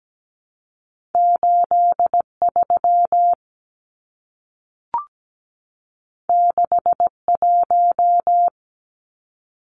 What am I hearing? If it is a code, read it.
Morse code, heard as a keyed tone, about 17 wpm, 700 Hz. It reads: 83 61
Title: Numbers - 2-Digit 17wpm